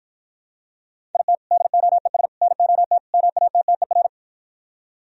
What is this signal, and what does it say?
Morse code, heard as a keyed tone, about 54 wpm, 700 Hz. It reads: IT DOES NOT MATTER